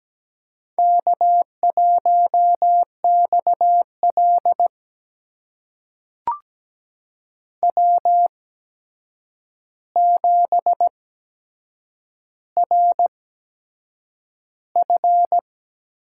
Morse code, heard as a keyed tone, about 17 wpm, spaced as for 7 wpm, 700 Hz. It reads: K1XL W7RF
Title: USA Call Signs - 4 Characters - 8x Character Spacing - 17wpm